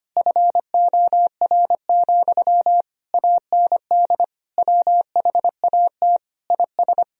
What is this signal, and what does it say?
Morse code, heard as a keyed tone, about 25 wpm, 700 Hz. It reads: FOR, AND WHAT IS
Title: Sentences from Top 1000 Words - Review Part 4 - 25wpm